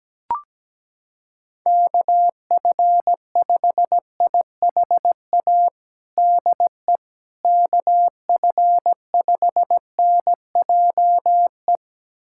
Morse code, heard as a keyed tone, about 17 wpm, 700 Hz. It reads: KF5IHA DE KF5NJE
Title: USA Callsign DE Callsign 17wpm